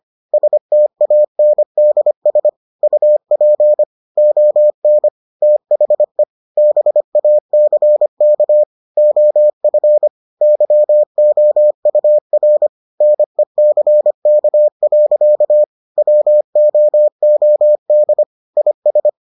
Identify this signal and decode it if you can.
Morse code, heard as a keyed tone, about 25 wpm, 600 Hz. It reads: STANDS UP ON THE BACK OF YOUR NECK. WOOD IS